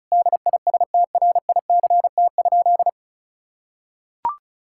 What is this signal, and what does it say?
Morse code, heard as a keyed tone, about 36 wpm, 700 Hz. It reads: DISTRICT?
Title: Sentences from Top 1600 Words 35wpm